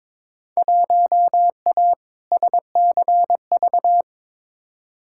Morse code, heard as a keyed tone, about 22 wpm, 700 Hz. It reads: 1A SCV
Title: Field Day Exchanges - 22wpm